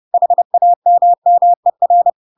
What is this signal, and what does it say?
Morse code, heard as a keyed tone, about 30 wpm, 700 Hz. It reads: HAMMER
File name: Pinocchio / Pinocchio ch4 30 30 ik1hge